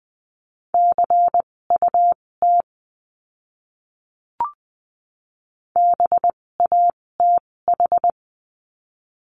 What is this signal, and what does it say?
Morse code, heard as a keyed tone, about 20 wpm, spaced as for 16 wpm, 700 Hz. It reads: CUT BATH
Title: 256 Lesson 42a - Words 12-20wpm